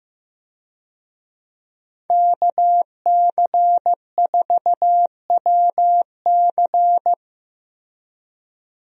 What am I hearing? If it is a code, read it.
Morse code, heard as a keyed tone, about 15 wpm, 700 Hz. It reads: KC4WC
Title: USA Call Signs - 5 Characters - Repeated 3x - 15wpm